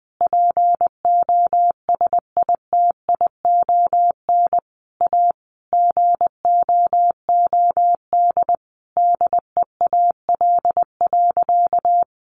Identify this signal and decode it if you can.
Morse code, heard as a keyed tone, about 20 wpm, 700 Hz. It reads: POSITION A GOOD DEAL.
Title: Sentences from Top 1200 Words 20wpm